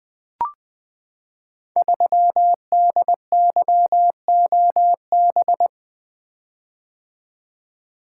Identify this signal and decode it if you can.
Morse code, heard as a keyed tone, about 20 wpm, 700 Hz. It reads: 3DYOB
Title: Code Groups - 5 Letter:Numbers - 20wpm